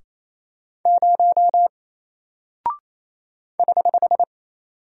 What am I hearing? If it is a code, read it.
Morse code, heard as a keyed tone, about 27 wpm, 700 Hz. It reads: 0 <HH>
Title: Single Character (Letter:Number:Punctuation:Prosign) 28wpm